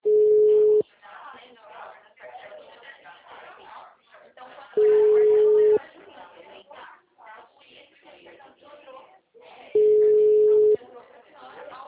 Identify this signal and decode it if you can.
Ringback tone (Europe, 425 Hz), with speech.